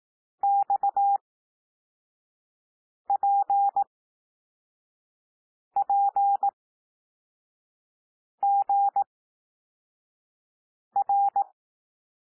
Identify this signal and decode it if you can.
Morse code, heard as a keyed tone, about 18 wpm, spaced as for 4 wpm, 800 Hz. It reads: XPPGR
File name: cwmp3 / L2 SOUND LETTERS 18 wpm